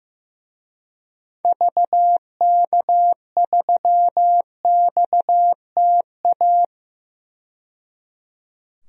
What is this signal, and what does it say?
Morse code, heard as a keyed tone, about 15 wpm, 700 Hz. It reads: VK3XTA
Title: DXCC-100 Call Signs - 6 Characters - 15wpm